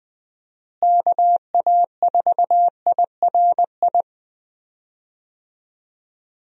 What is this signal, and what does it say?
Morse code, heard as a keyed tone, about 20 wpm, 700 Hz. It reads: KA4IRI
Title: USA Call Signs - 6 Characters - Repeated 2x - 20wpm